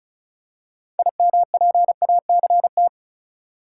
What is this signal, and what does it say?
Morse code, heard as a keyed tone, about 34 wpm, 700 Hz. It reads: IMPACT
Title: Six Letter Words - 35wpm